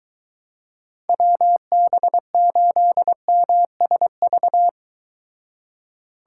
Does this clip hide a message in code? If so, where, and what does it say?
Yes: Morse code from 1.1 to 4.7 s, heard as a keyed tone, about 23 wpm, 700 Hz. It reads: WB8MSV